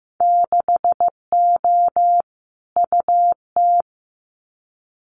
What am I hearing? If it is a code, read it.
Morse code, heard as a keyed tone, about 15 wpm, 700 Hz. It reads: 6O UT
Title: Winter Field Day Exchanges Encore 15wpm